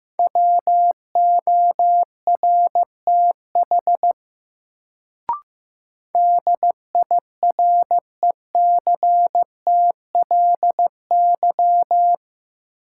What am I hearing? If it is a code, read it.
Morse code, heard as a keyed tone, about 15 wpm, 700 Hz. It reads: WORTH DIRECTLY
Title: Sentences from Top 800 Words - New Word Drill - 15wpm